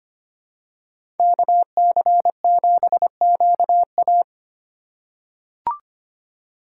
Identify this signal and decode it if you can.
Morse code, heard as a keyed tone, about 25 wpm, 700 Hz. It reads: KC7QA
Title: USA Call Signs - 5 Characters - 4x Character Spacing - 25wpm